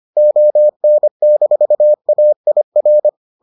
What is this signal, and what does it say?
Morse code, heard as a keyed tone, about 25 wpm, 600 Hz. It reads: ON-AIR